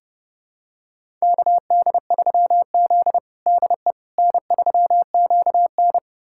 Morse code, heard as a keyed tone, about 30 wpm, 700 Hz. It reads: KD3Z DE N3QN
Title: USA Callsign DE Callsign - 4 Characters - 30wpm